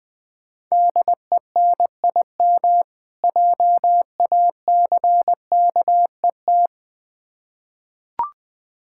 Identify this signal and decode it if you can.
Morse code, heard as a keyed tone, about 20 wpm, 700 Hz. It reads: DENIM JACKET
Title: Difficult Words with Hints - 20wpm